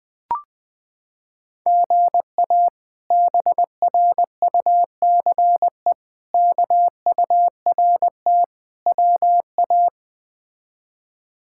Morse code, heard as a keyed tone, about 21 wpm, 700 Hz. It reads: GA BRUCE KURT WA
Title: K1USN SST QSO Practice - 20wpm